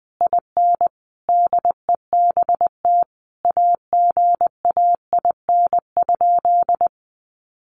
Morse code, heard as a keyed tone, about 20 wpm, 700 Hz. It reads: IN DEBT AGAIN?
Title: Sentences from Top 1300 Words 20wpm